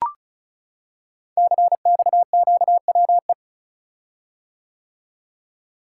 Morse code, heard as a keyed tone, about 36 wpm, 700 Hz. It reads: CXQWE
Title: Code Groups - 5 Letters - 35wpm